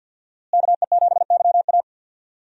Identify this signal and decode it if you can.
Morse code, heard as a keyed tone, about 50 wpm, 700 Hz. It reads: KE7YA
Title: USA Call Signs - 5 Characters - Repeated 3x - 50wpm